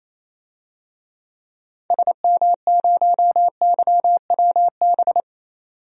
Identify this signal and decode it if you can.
Morse code, heard as a keyed tone, about 27 wpm, 700 Hz. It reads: SM0YWB
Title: DXCC-100 Call Signs - 6 Characters - 28wpm